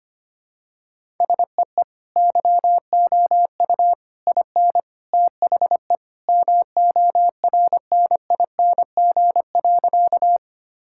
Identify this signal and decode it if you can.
Morse code, heard as a keyed tone, about 25 wpm, 700 Hz. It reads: SEE YOU IN THE MORNING.
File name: practice / 25wpm / Sentences with 3-Word Phrases - 25wpm